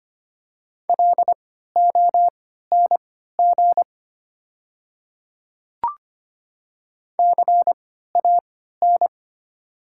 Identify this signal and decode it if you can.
Morse code, heard as a keyed tone, about 25 wpm, spaced as for 14 wpm, 700 Hz. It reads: LONG CAN
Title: 3x Character Spacing - Top 100 Words - 25wpm